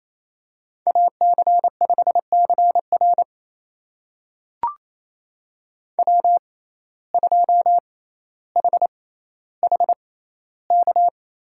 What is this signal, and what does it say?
Morse code, heard as a keyed tone, about 28 wpm, spaced as for 14 wpm, 700 Hz. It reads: AC5CR W2HHK
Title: USA Call Signs - 5 Characters - 6x Character Spacing - 28wpm